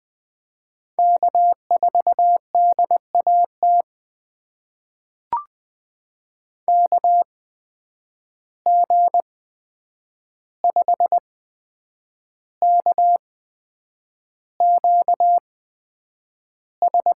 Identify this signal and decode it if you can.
Morse code, heard as a keyed tone, about 21 wpm, spaced as for 8 wpm, 700 Hz. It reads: K4DAT KG5KQS
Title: USA Call Signs - 8x Character Spacing - 20wpm